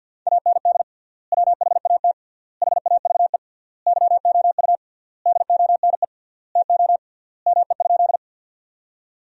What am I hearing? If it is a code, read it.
Morse code, heard as a keyed tone, about 53 wpm, 700 Hz. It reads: AND WHAT HAVE YOU DONE TO ME?
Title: 3x Word-Spacing - Sentences from Top 500 Words - 50wpm